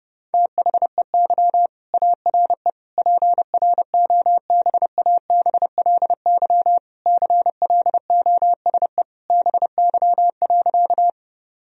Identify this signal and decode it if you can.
Morse code, heard as a keyed tone, about 29 wpm, 700 Hz. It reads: THEY ARE PROBABLY CLOSE BY.
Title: Sentences from Top 1000 Words - Review Part 4 - 30wpm